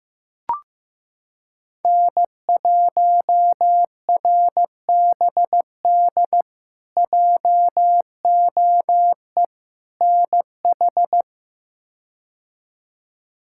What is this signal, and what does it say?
Morse code, heard as a keyed tone, about 15 wpm, 700 Hz. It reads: N1RBD JOE NH
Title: K1USN SST QSO Practice - 15wpm